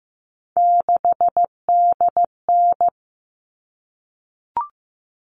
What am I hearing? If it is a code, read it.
Morse code, heard as a keyed tone, about 15 wpm, 700 Hz. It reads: 6DN
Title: All Cut Numbers - 3-Digit 15wpm